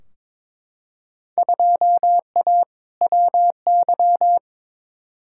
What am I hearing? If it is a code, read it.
Morse code, heard as a keyed tone, about 22 wpm, 700 Hz. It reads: 2A WY
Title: Field Day Exchanges - 22wpm